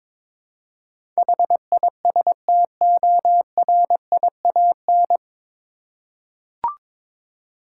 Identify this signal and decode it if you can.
Morse code, heard as a keyed tone, about 22 wpm, 700 Hz. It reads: HISTORIAN
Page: Morse Code Ninja practice 22wpm